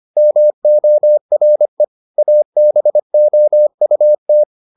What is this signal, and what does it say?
Morse code, heard as a keyed tone, about 25 wpm, 600 Hz. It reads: MORE ABOUT